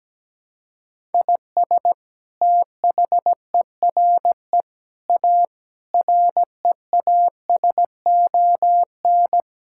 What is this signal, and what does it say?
Morse code, heard as a keyed tone, about 17 wpm, 700 Hz. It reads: IS THERE A REASON
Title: Sentences from Top 1000 Words - Review Part 5 - 17wpm